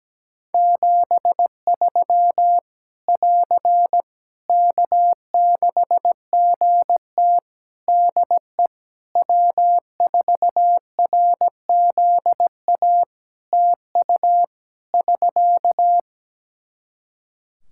Morse code, heard as a keyed tone, about 17 wpm, 700 Hz. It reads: 73 <AR> K6GT DE W4RZA TU <SK>
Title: Practice QSO - Exchange 4 - Ending - 17wpm